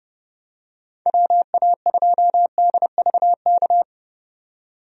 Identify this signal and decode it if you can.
Morse code, heard as a keyed tone, about 31 wpm, 700 Hz. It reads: WA2DVK
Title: USA Call Signs - 6 Characters - 30wpm